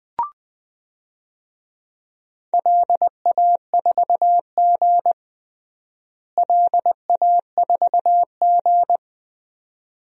Morse code, heard as a keyed tone, about 20 wpm, 700 Hz. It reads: LA4G LA4G
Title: DXCC-100 Call Signs - 4 Characters - Repeated 2x - 20wpm